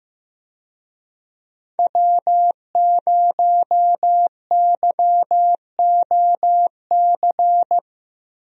Morse code, heard as a keyed tone, about 15 wpm, 700 Hz. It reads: W0YOC
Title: USA Call Signs - 5 Characters - 15wpm